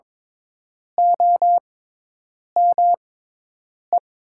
Morse code, heard as a keyed tone, about 22 wpm, spaced as for 8 wpm, 700 Hz. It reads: OME
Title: 6x Character Spacing - Top 500 Words - 22wpm